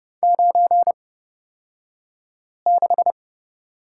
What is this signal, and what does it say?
Morse code, heard as a keyed tone, about 29 wpm, spaced as for 5 wpm, 700 Hz. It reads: 96
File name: Lesson 36d - BT - Two Characters 30wpm